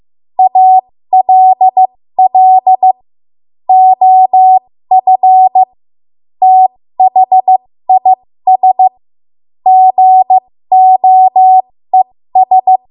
Morse code, heard as a keyed tone, about 15 wpm, spaced as for 13 wpm, 750 Hz. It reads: ALL OF THIS GOES
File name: Morse / 13WPM